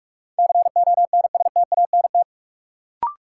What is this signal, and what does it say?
Morse code, heard as a keyed tone, about 46 wpm, 700 Hz. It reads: CONSTANT
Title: Sentences from Top 1500 Words - New Word Drill Alt - 45wpm